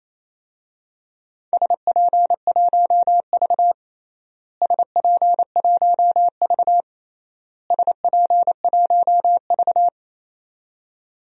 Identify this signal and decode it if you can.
Morse code, heard as a keyed tone, about 28 wpm, 700 Hz. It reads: SP1V SP1V SP1V
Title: DXCC-100 Call Signs - 4 Characters - Repeated 3x - 28wpm